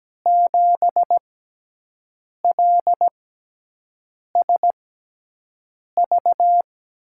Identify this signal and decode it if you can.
Morse code, heard as a keyed tone, about 17 wpm, spaced as for 6 wpm, 700 Hz. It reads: 7LSV
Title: USA Call Signs - 6 Characters - 6x Character Spacing - 17wpm